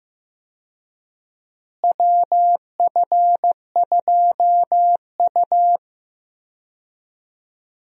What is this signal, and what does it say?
Morse code, heard as a keyed tone, about 15 wpm, 700 Hz. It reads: WF2U
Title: USA Call Signs - 4 Characters - Repeated 3x - 15wpm